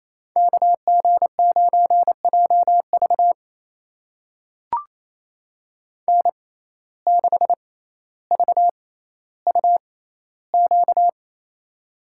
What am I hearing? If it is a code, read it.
Morse code, heard as a keyed tone, about 28 wpm, spaced as for 14 wpm, 700 Hz. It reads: KG9JV N6VUQ
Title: USA Call Signs - 5 Characters - 6x Character Spacing - 28wpm